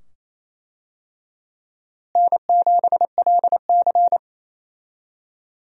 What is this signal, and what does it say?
Morse code, heard as a keyed tone, about 28 wpm, 700 Hz. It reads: N7LC
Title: USA Call Signs - 4 Characters - Repeated 2x - 28wpm